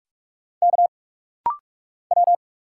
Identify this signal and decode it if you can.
Morse code, heard as a keyed tone, about 46 wpm, 700 Hz. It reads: K W
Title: Single Letter-Number - Rapid-Fire 45wpm